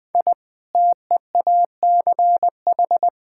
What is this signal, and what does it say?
Morse code, heard as a keyed tone, about 21 wpm, 700 Hz. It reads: I TEACH
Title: Sentences from Top 1000 Words - Review Part 6 - 20wpm